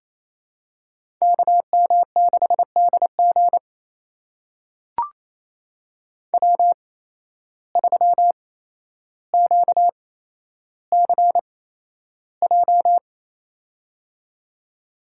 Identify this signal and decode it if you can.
Morse code, heard as a keyed tone, about 28 wpm, spaced as for 12 wpm, 700 Hz. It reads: KM6DG W3QCJ
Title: USA Call Signs - 8x Character Spacing - 28wpm